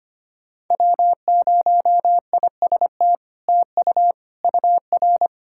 Morse code, heard as a keyed tone, about 25 wpm, 700 Hz. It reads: W0IST TU UR
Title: SOTA QSO Practice 25wpm